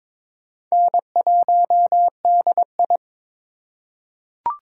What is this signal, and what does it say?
Morse code, heard as a keyed tone, about 22 wpm, 700 Hz. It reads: N1DI
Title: USA Call Signs - 4 Characters - 3x Character Spacing - 22wpm